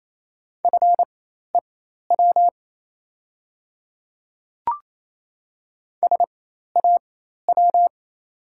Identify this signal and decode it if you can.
Morse code, heard as a keyed tone, about 28 wpm, spaced as for 13 wpm, 700 Hz. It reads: FEW SAW
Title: 4x Character Spacing - Top 300 Words - 28wpm